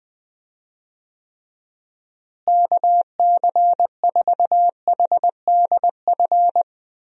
Morse code, heard as a keyed tone, about 21 wpm, 700 Hz. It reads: KC4HDF
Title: USA Call Signs - 6 Characters - Repeated 3x - 20wpm